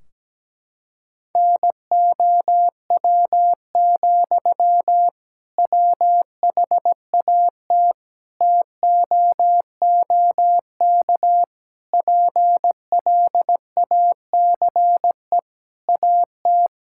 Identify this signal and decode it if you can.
Morse code, heard as a keyed tone, about 17 wpm, 700 Hz. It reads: NOW, WHAT TOOK PLACE AT
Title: Sentences from Top 1600 Words 17wpm